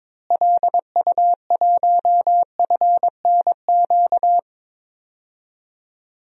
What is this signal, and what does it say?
Morse code, heard as a keyed tone, about 22 wpm, 700 Hz. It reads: LU1FNQ
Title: DXCC-100 Call Signs - 6 Characters - Repeated 2x - 22wpm